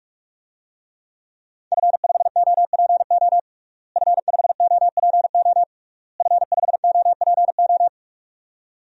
Morse code, heard as a keyed tone, about 46 wpm, 700 Hz. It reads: F5OPO F5OPO F5OPO
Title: DXCC-100 Call Signs - 5 Characters - Repeated 3x - 45wpm